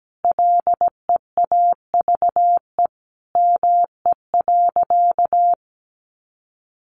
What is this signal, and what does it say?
Morse code, heard as a keyed tone, about 17 wpm, 700 Hz. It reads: LEAVE ME.
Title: Sentences from Top 600 Words 17wpm